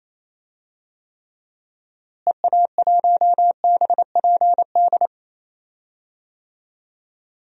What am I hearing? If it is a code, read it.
Morse code, heard as a keyed tone, about 28 wpm, 700 Hz. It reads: EA1BPD
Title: DXCC-100 Call Signs - 6 Characters - Repeated 2x - 28wpm